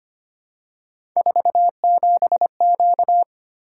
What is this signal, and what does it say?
Morse code, heard as a keyed tone, about 25 wpm, 700 Hz. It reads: 47Q
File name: Three Letter-Number Combinations - 25wpm